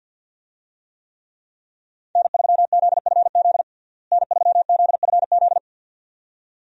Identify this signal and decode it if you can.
Morse code, heard as a keyed tone, about 49 wpm, 700 Hz. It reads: N3ZFZ N3ZFZ
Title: USA Call Signs - 5 Characters - Repeated 2x - 50wpm